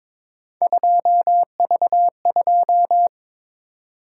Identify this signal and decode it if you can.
Morse code, heard as a keyed tone, about 26 wpm, 700 Hz. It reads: 2V2
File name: Speed-Racing - Three Letter-Number Combinations - 22wpm